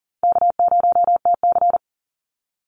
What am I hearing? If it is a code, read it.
Morse code, heard as a keyed tone, about 40 wpm, 700 Hz. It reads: K0TC
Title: Easy Call Signs - 40wpm